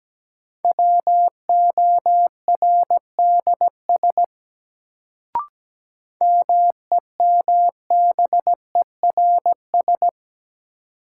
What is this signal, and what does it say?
Morse code, heard as a keyed tone, about 17 wpm, 700 Hz. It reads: WORDS MEMBERS
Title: Sets of 3 Words - New Word Drill - 17wpm